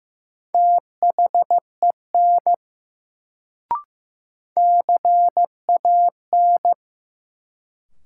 Morse code, heard as a keyed tone, about 15 wpm, 700 Hz. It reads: THEN CAN
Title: Sentences from Top 100 Words - New Word Drill - 15wpm